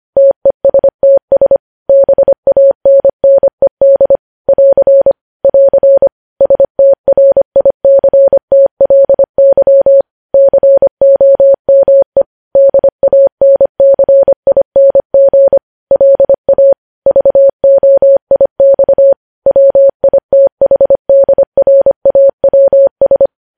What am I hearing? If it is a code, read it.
Morse code, heard as a keyed tone, about 25 wpm, 550 Hz. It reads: TESTS BANNED <AR> <AR> STRICTLY COME DANCING LA VOIX WITHDRAWS
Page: News Headlines in Morse Code at 25 WPM